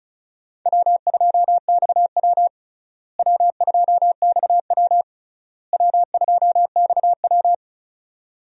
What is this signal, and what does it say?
Morse code, heard as a keyed tone, about 36 wpm, 700 Hz. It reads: W2XW W2XW W2XW
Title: USA Call Signs - 4 Characters - Repeated 3x - 35wpm